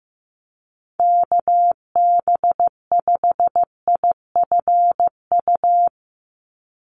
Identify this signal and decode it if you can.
Morse code, heard as a keyed tone, about 15 wpm, 700 Hz. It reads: KB5IFU
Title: USA Call Signs 15wpm